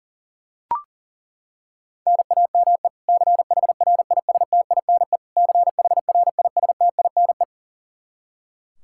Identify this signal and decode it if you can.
Morse code, heard as a keyed tone, about 40 wpm, 700 Hz. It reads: NAME CHRISTINE CHRISTINE
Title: Practice QSO - Exchange 1 - Beginning - 40wpm